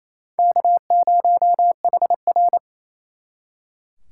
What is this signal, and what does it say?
Morse code, heard as a keyed tone, about 36 wpm, 700 Hz. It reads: K0HR